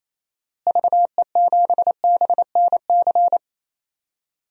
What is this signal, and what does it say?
Morse code, heard as a keyed tone, about 28 wpm, 700 Hz. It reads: VE7BNC
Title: DXCC-100 Call Signs - 6 Characters - Repeated 2x - 28wpm